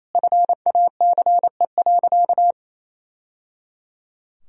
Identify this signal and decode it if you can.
Morse code, heard as a keyed tone, about 28 wpm, 700 Hz. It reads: FACE.
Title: Sentences from Top 1000 Words - Review Part 2 - 28wpm